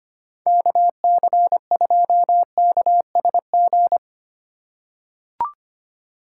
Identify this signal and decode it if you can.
Morse code, heard as a keyed tone, about 25 wpm, 700 Hz. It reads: KC2KSG
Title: USA Call Signs - 6 Characters - 4x Character Spacing - 25wpm